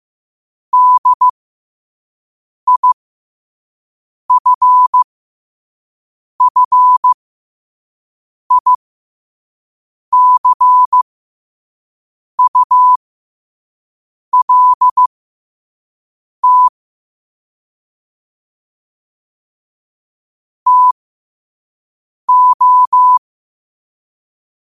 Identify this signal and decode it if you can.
Morse code, heard as a keyed tone, about 15 wpm, spaced as for 5 wpm, 1000 Hz. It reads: DIFFICULT TO
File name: qotdfiles / 5wpm